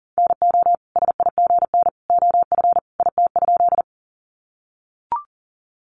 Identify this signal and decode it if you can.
Morse code, heard as a keyed tone, about 40 wpm, 700 Hz. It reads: NO SIGN OF IT?
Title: Sentences from Top 1000 Words 40wpm